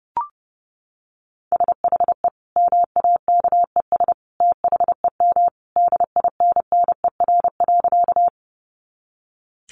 Morse code, heard as a keyed tone, about 30 wpm, 700 Hz. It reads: SHE MAKES THEM DINNER.
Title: Easy Phrases 30wpm